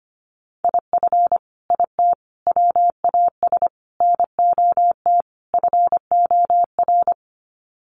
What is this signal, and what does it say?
Morse code, heard as a keyed tone, about 25 wpm, 700 Hz. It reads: IF IT WAS NOT FOR
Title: Sets of 5 Words Encore 25wpm